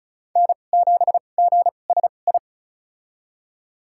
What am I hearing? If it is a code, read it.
Morse code, heard as a keyed tone, about 34 wpm, spaced as for 25 wpm, 700 Hz. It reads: N7GSI
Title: USA Call Signs - 5 Characters - 2x Character Spacing - 35wpm